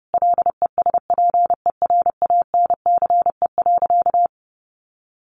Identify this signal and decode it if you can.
Morse code, heard as a keyed tone, about 30 wpm, 700 Hz. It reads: LESPERANCE.
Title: A Sound Of Thunder 30wpm